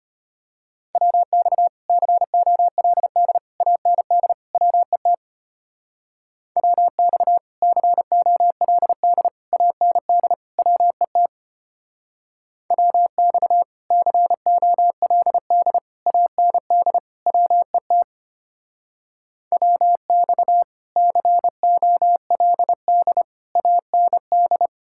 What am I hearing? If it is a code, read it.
Morse code, heard as a keyed tone, about 32 wpm, 700 Hz. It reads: WX COLD AND WET WX COLD AND WET WX COLD AND WET WX COLD AND